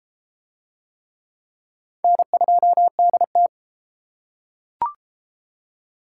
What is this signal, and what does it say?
Morse code, heard as a keyed tone, about 31 wpm, 700 Hz. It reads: N2DT